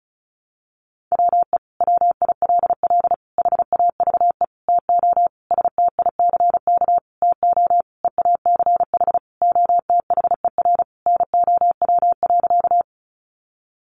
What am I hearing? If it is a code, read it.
Morse code, heard as a keyed tone, about 34 wpm, 700 Hz. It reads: WE WILL HAVE TO STICK TO EACH OTHER NOW.
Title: Sentences from Top 1200 Words 35wpm